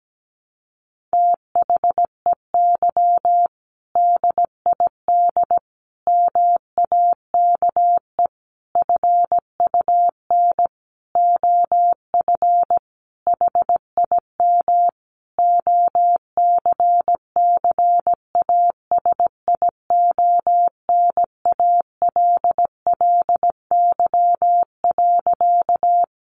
Morse code, heard as a keyed tone, about 17 wpm, 700 Hz. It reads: THEY DID MAKE FUN OF HIM OCCASIONALLY.
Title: Sentences from Top 1300 Words 17wpm